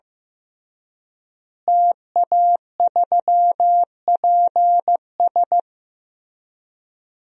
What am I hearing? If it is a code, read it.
Morse code, heard as a keyed tone, about 14 wpm, 700 Hz. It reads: TA3PS
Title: DXCC-100 Call Signs - 5 Characters - Repeated 3x - 15wpm